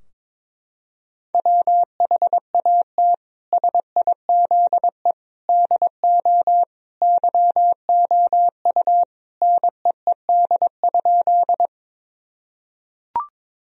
Morse code, heard as a keyed tone, about 22 wpm, 700 Hz. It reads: WHAT SIZE DO YOU NEED?
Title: Sentences from Top 1000 Words - Review Part 2 - 22wpm